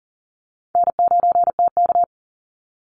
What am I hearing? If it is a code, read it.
Morse code, heard as a keyed tone, about 40 wpm, 700 Hz. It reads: N9TK
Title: Easy Call Signs Repeated 2x - 40wpm